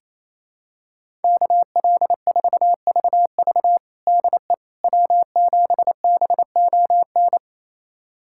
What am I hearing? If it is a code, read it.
Morse code, heard as a keyed tone, about 28 wpm, 700 Hz. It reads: KL4VV DE W7BON
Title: USA Callsign DE Callsign - 5 Characters - 28wpm